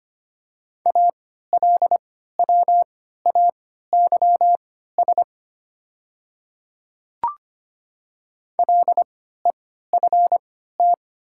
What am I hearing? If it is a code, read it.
Morse code, heard as a keyed tone, about 25 wpm, spaced as for 14 wpm, 700 Hz. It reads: ALWAYS LEFT
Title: 3x Character Spacing - Top 300 Words - 25wpm